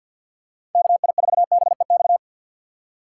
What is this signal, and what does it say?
Morse code, heard as a keyed tone, about 52 wpm, 700 Hz. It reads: KI4BEX
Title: USA Call Signs - 6 Characters - 4x Character Spacing - 50wpm